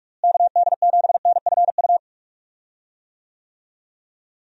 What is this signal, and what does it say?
Morse code, heard as a keyed tone, about 46 wpm, 700 Hz. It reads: KD7NFU
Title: USA Call Signs - 6 Characters - Repeated 2x - 45wpm